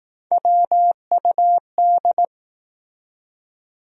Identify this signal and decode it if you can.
Morse code, heard as a keyed tone, about 21 wpm, 700 Hz. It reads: WUD